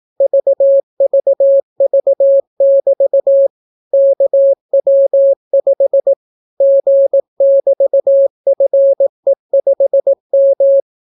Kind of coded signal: Morse code